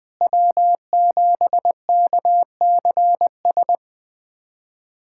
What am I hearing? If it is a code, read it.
Morse code, heard as a keyed tone, about 20 wpm, 700 Hz. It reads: W7KCS
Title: USA Callsign DE Callsign - 5 Characters - 20wpm